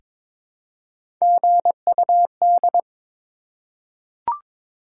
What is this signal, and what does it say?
Morse code, heard as a keyed tone, about 22 wpm, 700 Hz. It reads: GUD
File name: Top 20 QSO Elements - 3x Character Spacing - 22wpm